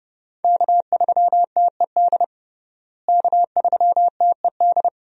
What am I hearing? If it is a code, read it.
Morse code, heard as a keyed tone, about 32 wpm, 700 Hz. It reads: K3TED K3TED
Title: USA Call Signs - 5 Characters - Repeated 2x - 30wpm